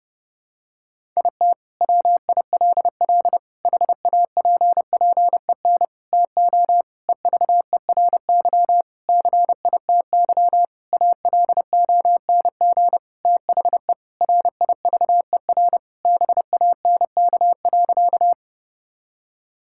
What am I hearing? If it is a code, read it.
Morse code, heard as a keyed tone, about 30 wpm, 700 Hz. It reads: IT WILL HAPPEN TO EVERY CITY ALONG THE RIVER BANK.
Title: Sentences from Top 1000 Words - Review Part 6 - 30wpm